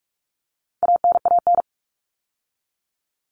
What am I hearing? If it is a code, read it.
Morse code, heard as a keyed tone, about 44 wpm, 700 Hz. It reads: ANAN
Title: All Cut Numbers - 4-Digit 45wpm